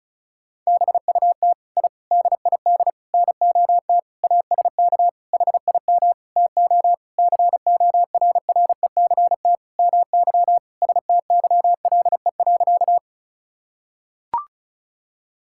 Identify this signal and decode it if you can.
Morse code, heard as a keyed tone, about 34 wpm, 700 Hz. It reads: BUT I DID NOT ASK HIM TO CORRECT MY STYLE.
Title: Sentences from Top 1000 Words - Review Part 5 - 35wpm